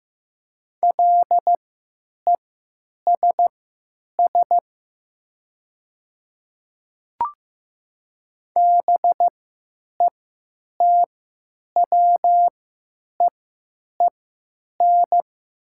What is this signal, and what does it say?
Morse code, heard as a keyed tone, about 15 wpm, spaced as for 9 wpm, 700 Hz. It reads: LESS BETWEEN
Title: 3x Character Spacing - Top 500 Words - 15wpm